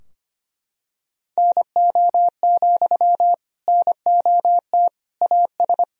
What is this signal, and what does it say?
Morse code, heard as a keyed tone, about 25 wpm, 700 Hz. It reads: NO, NOT AS
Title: Sentences from Top 1000 Words - Review Part 4 - 25wpm